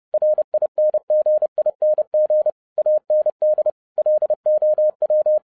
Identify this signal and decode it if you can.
Morse code, heard as a keyed tone, about 30 wpm, 610 Hz. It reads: RINGING AND LOW
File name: 2011-07-mp3 / n60 BACK FROM A BIG ACCIDENT 30wpm